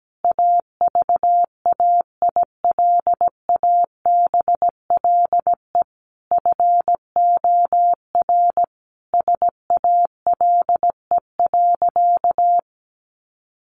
Morse code, heard as a keyed tone, about 17 wpm, 700 Hz. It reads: AVAILABLE FOR SALE.
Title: Sentences from Top 1400 Words 17wpm